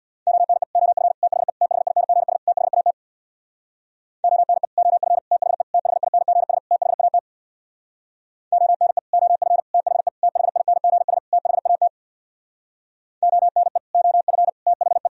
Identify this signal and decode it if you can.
Morse code, heard as a keyed tone, about 64 wpm, 700 Hz. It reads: ONE OF THE THINGS THAT ONE OF THE THINGS THAT ONE OF THE THINGS THAT ONE OF THE